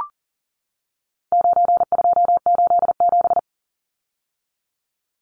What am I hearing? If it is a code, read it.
Morse code, heard as a keyed tone, about 40 wpm, 700 Hz. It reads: 9287
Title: Numbers - 4-Digit 40wpm